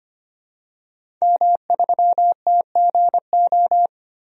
Morse code, heard as a keyed tone, about 25 wpm, 700 Hz. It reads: M3TGO
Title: DXCC-100 Call Signs - 5 Characters - 25wpm